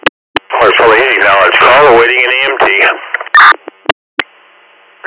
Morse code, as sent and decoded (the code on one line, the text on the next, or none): none